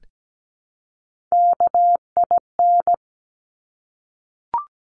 Morse code, heard as a keyed tone, about 17 wpm, 700 Hz. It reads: KIN
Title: Top Three Letter Combinations in Words 17wpm